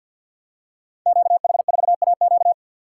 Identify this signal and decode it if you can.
Morse code, heard as a keyed tone, about 52 wpm, 700 Hz. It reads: QH4AQ
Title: Code Groups - 5 Letter:Numbers - 50wpm